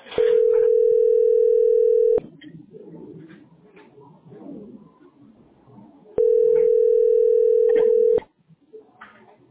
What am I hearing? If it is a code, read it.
Ringback tone (North America), 440 + 480 Hz.